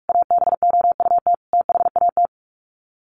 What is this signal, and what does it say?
Morse code, heard as a keyed tone, about 44 wpm, 700 Hz. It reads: ABOUT THAT
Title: Sets of 5 Words Encore2 45wpm